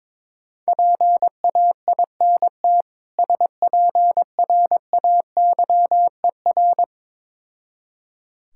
Morse code, heard as a keyed tone, about 22 wpm, 690 Hz. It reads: PAINT SPRAYER
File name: Difficult Words with Hints - 22wpm